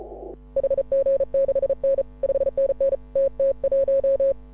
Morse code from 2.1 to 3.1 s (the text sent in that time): 5NN